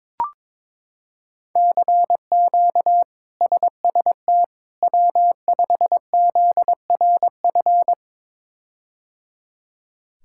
Morse code, heard as a keyed tone, about 22 wpm, 700 Hz. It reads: CQ SST W5ZRF
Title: K1USN SST QSO Practice - 22wpm